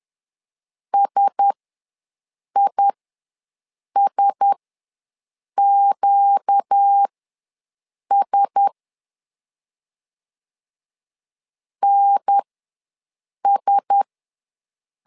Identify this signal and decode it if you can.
Morse code, heard as a keyed tone, about 10 wpm, spaced as for 6 wpm, 800 Hz. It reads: SISQS NS
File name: LEZIONI / 4 Q1S ws 10wpm 2208